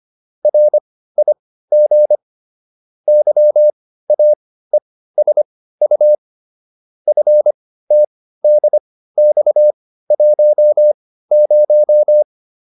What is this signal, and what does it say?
Morse code, heard as a keyed tone, about 25 wpm, spaced as for 15 wpm, 600 Hz. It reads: RIG YAESU FTDX10